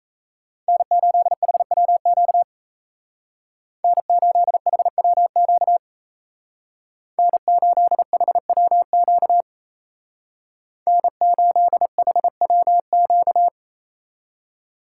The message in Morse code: N8HWQ N8HWQ N8HWQ N8HWQ

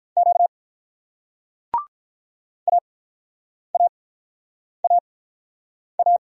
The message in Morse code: Q AAAA

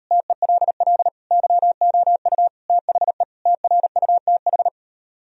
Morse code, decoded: TELL YOU THE TRUTH

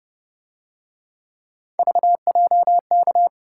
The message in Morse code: VJK

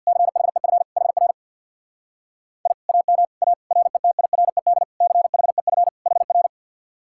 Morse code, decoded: CHEF HR I AM A RETIRED CHEF HR

